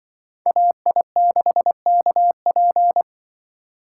Morse code, heard as a keyed tone, about 24 wpm, 700 Hz. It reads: AI6KP
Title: Speed Racing - USA 2x2 Call Signs 20wpm